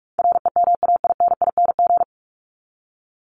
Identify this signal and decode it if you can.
Morse code, heard as a keyed tone, about 44 wpm, 700 Hz. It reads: REMAINING
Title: ING Suffix - Top 100 Words with Base Words 45wpm